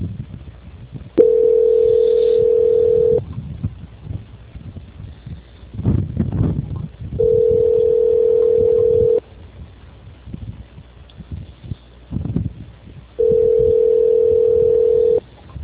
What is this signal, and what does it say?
Ringback tone (North America), 440 + 480 Hz.
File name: out- -2036- - - .53